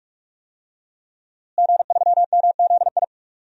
Morse code, heard as a keyed tone, about 42 wpm, 700 Hz. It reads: G3MZI
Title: DXCC-100 Call Signs - 5 Characters - Repeated 3x - 45wpm